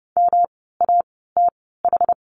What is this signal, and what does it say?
Morse code, heard as a keyed tone, about 30 wpm, spaced as for 17 wpm, 700 Hz. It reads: MATH
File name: course / 30:10wpm / 108 Lesson 18f - Letter m - Words 30wpm